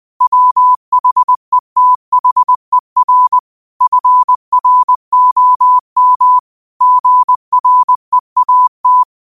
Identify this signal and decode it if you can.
Morse code, heard as a keyed tone, about 20 wpm, 1000 Hz. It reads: WHETHER FROM GREAT
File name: qotdfiles / 20wpm